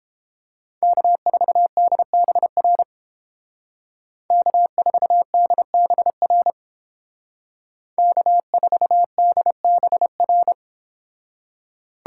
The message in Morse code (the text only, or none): K4DBR K4DBR K4DBR